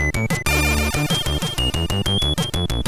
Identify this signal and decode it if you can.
There is no code.